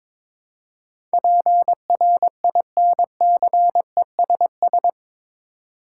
Morse code, heard as a keyed tone, about 22 wpm, 700 Hz. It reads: PRINCESS
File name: practice / 22wpm / Eight Letter Words - 22wpm